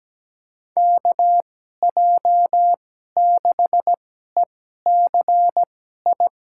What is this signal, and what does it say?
Morse code, heard as a keyed tone, about 17 wpm, spaced as for 12 wpm, 700 Hz. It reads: KJ6ECI
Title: USA Call Signs - 6 Characters - 2x Character Spacing - 17wpm